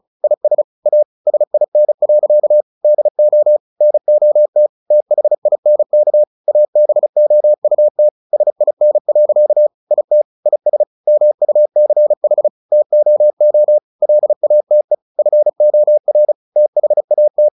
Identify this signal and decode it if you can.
Morse code, heard as a keyed tone, about 34 wpm, 600 Hz. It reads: IS A SIN. DO NOT THINK ABOUT SIN. IT IS MUCH TOO LATE FOR THAT
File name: the old man and the sea 2 35wpm